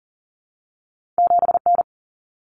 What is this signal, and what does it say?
Morse code, heard as a keyed tone, about 40 wpm, 700 Hz. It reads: 7N
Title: Common Cut Numbers - 2-Digit 40wpm